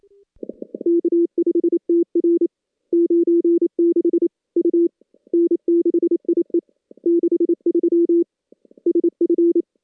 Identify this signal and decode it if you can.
Morse code, heard as a keyed tone, about 27 wpm, 340 Hz. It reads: K5TR 96 U N6IE 63 SF
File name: contests / CW-SS-k5tr- -2300